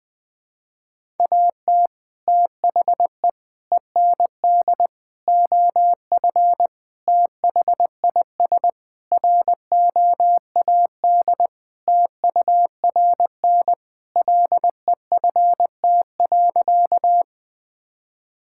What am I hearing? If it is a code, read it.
Morse code, heard as a keyed tone, about 20 wpm, 700 Hz. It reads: AT THE END OF THIS ROAD TURN LEFT.
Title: Sentences with 5-Word Phrases - 20wpm